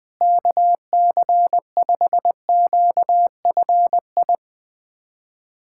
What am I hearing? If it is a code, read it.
Morse code, heard as a keyed tone, about 21 wpm, 700 Hz. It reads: KC5QFI
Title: USA Callsign DE Callsign - 6 Characters - 20wpm